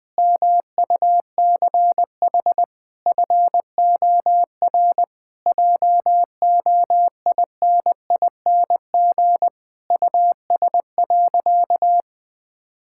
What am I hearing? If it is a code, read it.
Morse code, heard as a keyed tone, about 20 wpm, 700 Hz. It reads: MUCH FOR JOINING US.
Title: Sentences with 5-Word Phrases - Spoken Beginning - 20wpm